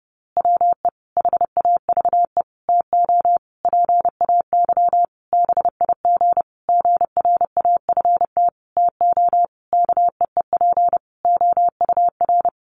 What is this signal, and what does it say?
Morse code, heard as a keyed tone, about 30 wpm, 700 Hz. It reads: WE HAVE TO PAY BIG GRAFT TO KEEP OUR
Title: A Sound Of Thunder 30wpm